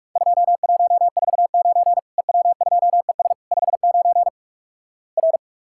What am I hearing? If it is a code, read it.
Morse code, heard as a keyed tone, about 45 wpm, 700 Hz. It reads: 2149 EW2ES 59 R
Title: Worked All Europe DX Contest - QTC Exchange - 45wpm